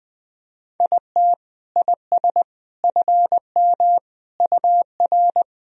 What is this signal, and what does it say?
Morse code, heard as a keyed tone, about 20 wpm, 700 Hz. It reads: IT IS FM UR
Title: Sentences from Top 300 Words and Abbreviations 20wpm